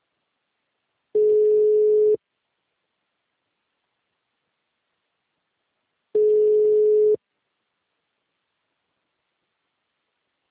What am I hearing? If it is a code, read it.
Ringback tone (Europe, 425 Hz).
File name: out- -2004- - - .26067